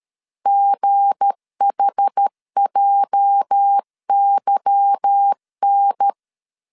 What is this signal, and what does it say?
Morse code, heard as a keyed tone, about 13 wpm, 800 Hz. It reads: GHJYN
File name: LEZIONI / 8 YJ45 / 8 YJ 12wpm 2235